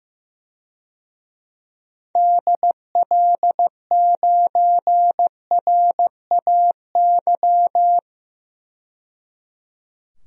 Morse code, heard as a keyed tone, about 16 wpm, 700 Hz. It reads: DL9RAY
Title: DXCC-100 Call Signs - 6 Characters - 15wpm